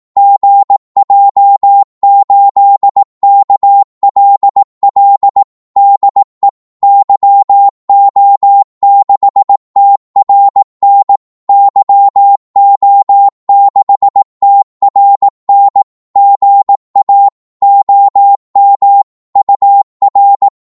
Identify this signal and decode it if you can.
Morse code, heard as a keyed tone, about 18 wpm, 800 Hz. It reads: GJ8KLL DE YO6TRN YO6TRN GA OM UR
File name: Morse / qso-18wpm